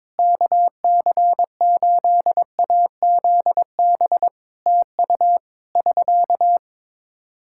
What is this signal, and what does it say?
Morse code, heard as a keyed tone, about 22 wpm, 700 Hz. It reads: KC8AZB TU <SK>
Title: K1USN SST QSO Practice 22wpm